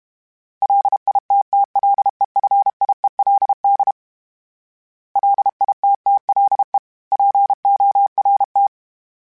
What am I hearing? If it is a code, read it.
Morse code, heard as a keyed tone, about 32 wpm, 800 Hz. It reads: LITTLEFIELD LITTLE PORT
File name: fromSMs / FromSMs mp3s / Cities@30